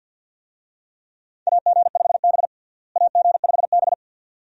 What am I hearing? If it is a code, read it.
Morse code, heard as a keyed tone, about 47 wpm, 700 Hz. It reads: AG5B AG5B
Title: USA Call Signs - 4 Characters - Repeated 2x - 50wpm